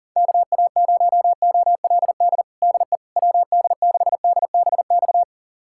Morse code, heard as a keyed tone, about 40 wpm, 700 Hz. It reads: KA0OLD DE WD6DBX
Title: USA Callsign DE Callsign - 6 Characters - 40wpm